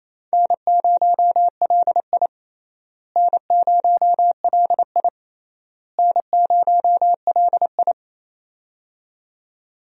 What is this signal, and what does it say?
Morse code, heard as a keyed tone, about 28 wpm, 700 Hz. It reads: N0LI N0LI N0LI
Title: USA Call Signs - 4 Characters - Repeated 3x - 28wpm